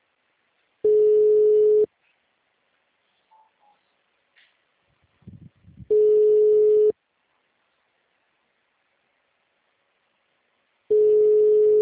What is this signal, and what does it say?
Ringback tone (Europe, 425 Hz).
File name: out- -2012- - - .4007